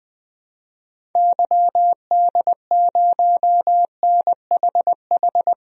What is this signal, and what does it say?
Morse code, heard as a keyed tone, about 20 wpm, 700 Hz. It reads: YD0NHH
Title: DXCC-100 Call Signs - 6 Characters - Repeated 2x - 20wpm